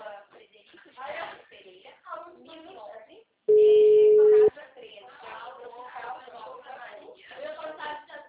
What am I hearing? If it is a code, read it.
Ringback tone (Europe, 425 Hz), with speech.